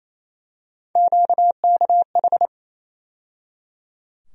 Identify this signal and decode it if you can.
Morse code, heard as a keyed tone, about 28 wpm, 700 Hz. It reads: QKH
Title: USA Call Sign Postfixes - 3 Letters 28wpm